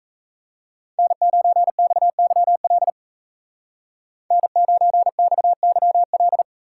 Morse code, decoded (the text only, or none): N9XYL N9XYL